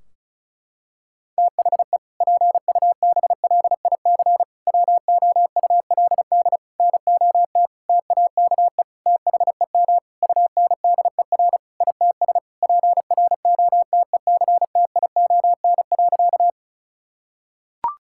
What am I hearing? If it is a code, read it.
Morse code, heard as a keyed tone, about 36 wpm, 700 Hz. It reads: THE PUBLIC WOULD NOT TAKE THEM UNDER ITS PROTECTION.
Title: Sentences from Top 1600 Words 35wpm